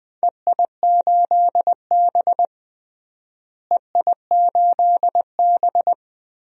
Morse code, heard as a keyed tone, about 20 wpm, 700 Hz. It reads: EI8B EI8B
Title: DXCC-100 Call Signs - 4 Characters - Repeated 3x - 20wpm